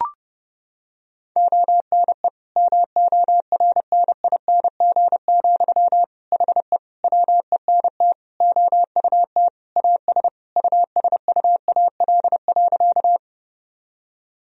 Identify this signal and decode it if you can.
Morse code, heard as a keyed tone, about 30 wpm, 700 Hz. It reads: ONE MORNING, HE WENT OUT AS USUAL.
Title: Sentences from Top 1000 Words - Review Part 2 - 30wpm